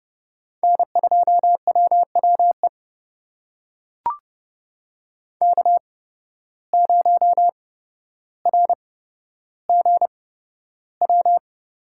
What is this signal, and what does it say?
Morse code, heard as a keyed tone, about 32 wpm, spaced as for 13 wpm, 700 Hz. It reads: N2WWE K0RGW
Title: USA Call Signs - 5 Characters - 8x Character Spacing - 30wpm